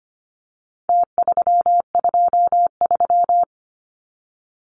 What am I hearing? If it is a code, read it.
Morse code, heard as a keyed tone, about 25 wpm, 700 Hz. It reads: T323
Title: Numbers - Contest Style - 4-Digit 25wpm